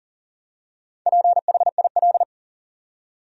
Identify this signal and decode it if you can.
Morse code, heard as a keyed tone, about 40 wpm, 700 Hz. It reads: PHIL
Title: USA Common Names - Max 4 Letters - 40wpm